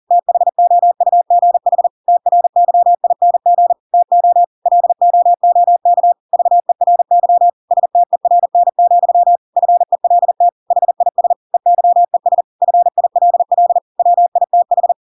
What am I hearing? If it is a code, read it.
Morse code, heard as a keyed tone, about 40 wpm, 700 Hz. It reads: THOUGH TRYING TO LOOK VERY STERN, FELT HIS EYES FILL WITH
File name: Pinocchio / Pinocchio ch8 40 40 ik1hge